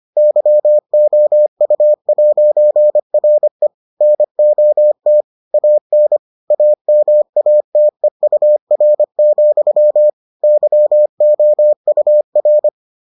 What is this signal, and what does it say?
Morse code, heard as a keyed tone, about 25 wpm, 600 Hz. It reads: YOU'RE NOT AN AMATEUR, YOUR